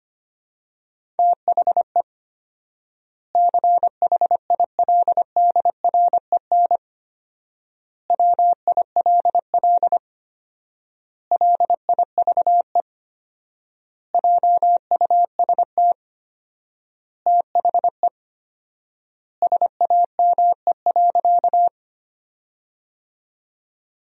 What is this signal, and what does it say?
Morse code, heard as a keyed tone, about 25 wpm, 700 Hz. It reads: THE CHILDREN WILL LIVE JUST THE SAME.
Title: 4x Word-Spacing - Sentences from Top 300 Words - 25wpm